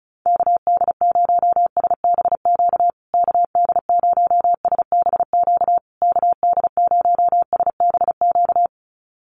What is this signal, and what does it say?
Morse code, heard as a keyed tone, about 34 wpm, 700 Hz. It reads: KD0SBQ KD0SBQ KD0SBQ
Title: USA Call Signs Repeated 3x 35wpm